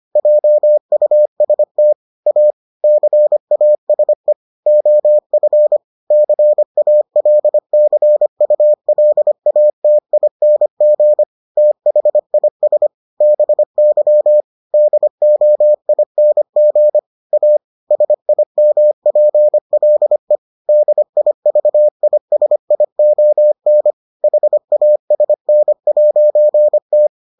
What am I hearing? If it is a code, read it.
Morse code, heard as a keyed tone, about 25 wpm, 600 Hz. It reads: JUST A CASE OF CALCULATING THIS BY DOING A SIMPLE DIVISION HASN'T